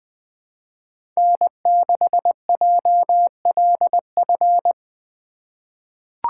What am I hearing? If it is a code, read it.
Morse code, heard as a keyed tone, about 20 wpm, 700 Hz. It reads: N6JLF
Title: USA Call Signs - 5 Characters - 4x Character Spacing - 20wpm